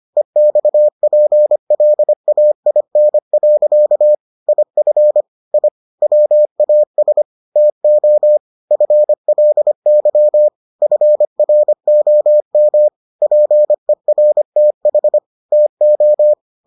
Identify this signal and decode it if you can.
Morse code, heard as a keyed tone, about 25 wpm, 600 Hz. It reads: EXPLAIN. IF I WAS TO FLY FROM PERTH TO